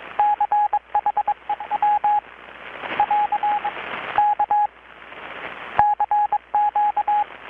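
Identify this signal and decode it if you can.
Morse code, heard as a keyed tone, about 22 wpm, 830 Hz, with radio noise. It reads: CHEW <AR> K CQ
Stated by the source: Morse from KPH's Night of Nights XX maritime radiotelegraphy event, received in Chiapas, Mexico about 3,800 km away on a Kenwood R-600 with a grounded 10 m long-wire antenna.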